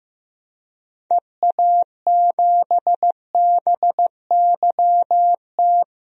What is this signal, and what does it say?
Morse code, heard as a keyed tone, about 15 wpm, 700 Hz. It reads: EA7BYT
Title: DXCC-100 Call Signs - 6 Characters - 15wpm